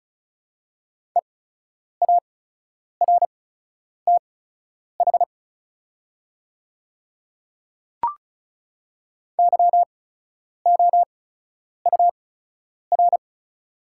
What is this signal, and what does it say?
Morse code, heard as a keyed tone, about 34 wpm, spaced as for 10 wpm, 690 Hz. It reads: EARTH YOUR
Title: 8x Character Spacing - Top 200 Words - 35wpm